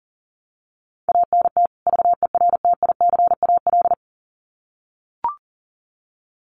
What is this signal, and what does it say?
Morse code, heard as a keyed tone, about 40 wpm, 700 Hz. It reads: ANT VERTICAL
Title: Common QSO Phrases 40wpm